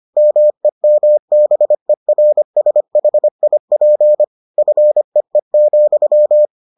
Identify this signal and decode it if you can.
Morse code, heard as a keyed tone, about 25 wpm, 600 Hz. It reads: MEMBERSHIP FEE,